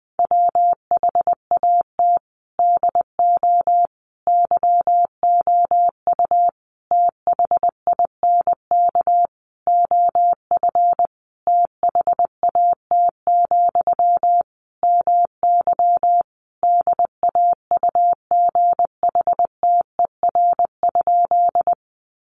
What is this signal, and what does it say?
Morse code, heard as a keyed tone, about 20 wpm, 700 Hz. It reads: WHAT DO YOU THINK OF THAT, MY DAUGHTER?
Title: Sentences from Top 1500 Words 20wpm